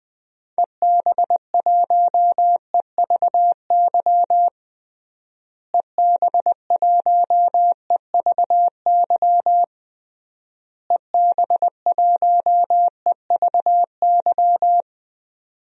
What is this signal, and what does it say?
Morse code, heard as a keyed tone, about 21 wpm, 700 Hz. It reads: EB1EVY EB1EVY EB1EVY
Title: DXCC-100 Call Signs - 6 Characters - Repeated 3x - 20wpm